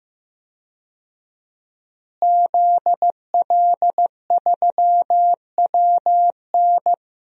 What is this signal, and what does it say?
Morse code, heard as a keyed tone, about 15 wpm, 700 Hz. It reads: ZL3WN
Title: DXCC-100 Call Signs - 5 Characters - Repeated 3x - 15wpm